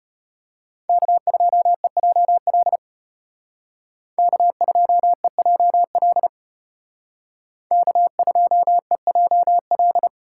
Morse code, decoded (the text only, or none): K2EJL K2EJL K2EJL